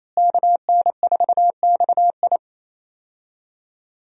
Morse code, heard as a keyed tone, about 28 wpm, 700 Hz. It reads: KN4XI
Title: USA Call Signs - 5 Characters - Repeated 3x - 28wpm